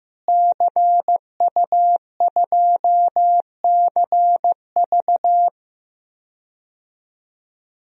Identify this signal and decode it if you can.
Morse code, heard as a keyed tone, about 15 wpm, 700 Hz. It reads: CU2CV
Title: DXCC-100 Call Signs - 5 Characters - Repeated 3x - 15wpm